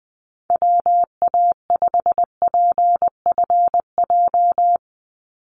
Morse code, heard as a keyed tone, about 20 wpm, 700 Hz. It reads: WA5PFJ
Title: Call Signs - 20wpm